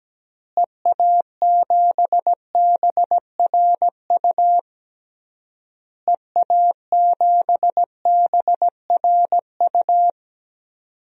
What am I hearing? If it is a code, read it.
Morse code, heard as a keyed tone, about 17 wpm, 700 Hz. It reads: EA7BRU EA7BRU
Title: DXCC-100 Call Signs - 6 Characters - Repeated 3x - 17wpm